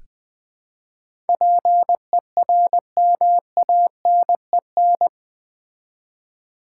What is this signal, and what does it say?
Morse code, heard as a keyed tone, about 20 wpm, 700 Hz. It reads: PERMANEN
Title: Build A Word - 4 Letters At A Time - 11-Letter Words - 20wpm